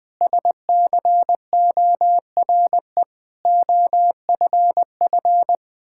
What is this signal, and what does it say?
Morse code, heard as a keyed tone, about 20 wpm, 700 Hz. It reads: SCORE OFF
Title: Sentences from Top 1600 Words 20wpm